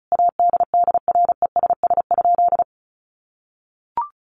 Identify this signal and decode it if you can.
Morse code, heard as a keyed tone, about 34 wpm, 700 Hz. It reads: ADDRESS?
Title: Sentences from Top 1000 Words 35wpm